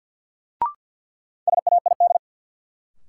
Morse code, heard as a keyed tone, about 54 wpm, 700 Hz. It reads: SAID
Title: Sentences from Top 100 Words - New Word Drill - 50wpm